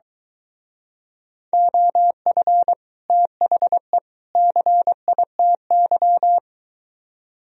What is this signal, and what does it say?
Morse code, heard as a keyed tone, about 23 wpm, 700 Hz. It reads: OF THE CITY